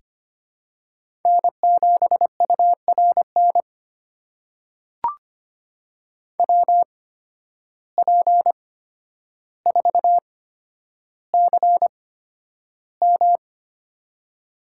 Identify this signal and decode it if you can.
Morse code, heard as a keyed tone, about 25 wpm, spaced as for 11 wpm, 700 Hz. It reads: N7URN WP4CM
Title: USA Call Signs - 5 Characters - 8x Character Spacing - 25wpm